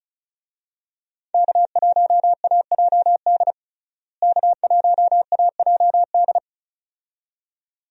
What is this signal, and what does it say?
Morse code, heard as a keyed tone, about 36 wpm, 700 Hz. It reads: K1AJD K1AJD
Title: USA Call Signs - 5 Characters - Repeated 2x - 35wpm